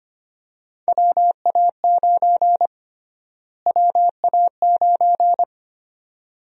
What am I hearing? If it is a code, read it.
Morse code, heard as a keyed tone, about 25 wpm, 700 Hz. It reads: WA9 WA9
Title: USA Call Sign Prefixes - Repeated 2x - 25wpm